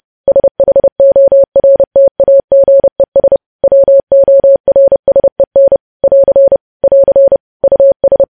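Morse code, heard as a keyed tone, about 30 wpm, 550 Hz. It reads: SHORTAGES WORSEN <AR> <AR> US
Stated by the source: Morse news headlines at 30 wpm.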